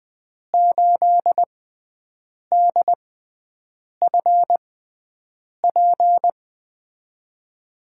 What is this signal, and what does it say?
Morse code, heard as a keyed tone, about 20 wpm, spaced as for 7 wpm, 700 Hz. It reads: 8DFP
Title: USA Call Signs - 6x Character Spacing - 20wpm